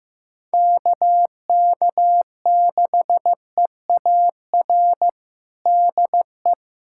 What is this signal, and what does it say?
Morse code, heard as a keyed tone, about 15 wpm, 700 Hz. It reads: KK6EAR DE
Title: USA Callsign DE Callsign 15wpm